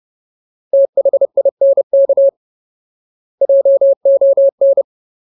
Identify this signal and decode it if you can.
Morse code, heard as a keyed tone, about 30 wpm, 550 Hz. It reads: THINK JON
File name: Ultimate Word List / 30WPM Ultimate